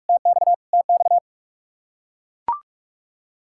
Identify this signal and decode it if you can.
Morse code, heard as a keyed tone, about 41 wpm, 700 Hz. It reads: TX TX
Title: K1USN SST QSO Practice 45wpm